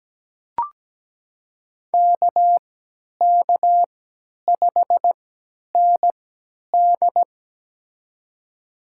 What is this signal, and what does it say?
Morse code, heard as a keyed tone, about 17 wpm, spaced as for 10 wpm, 700 Hz. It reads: KK5ND
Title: USA Call Signs - 3x Character Spacing - 17wpm